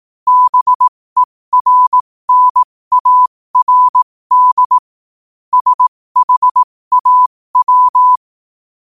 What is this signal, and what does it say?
Morse code, heard as a keyed tone, about 18 wpm, spaced as for 15 wpm, 1000 Hz. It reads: BERNARD SHAW
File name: qotdfiles / 15wpm